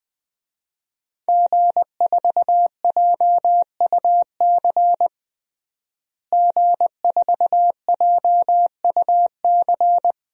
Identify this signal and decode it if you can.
Morse code, heard as a keyed tone, about 21 wpm, 700 Hz. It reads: G4JUC G4JUC
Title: DXCC-100 Call Signs - 5 Characters - Repeated 3x - 20wpm